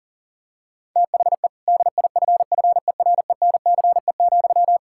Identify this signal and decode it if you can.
Morse code, heard as a keyed tone, about 40 wpm, 700 Hz. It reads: THE DIFFERENCE,
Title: Sentences from Top 1000 Words - Review Part 5 - 40wpm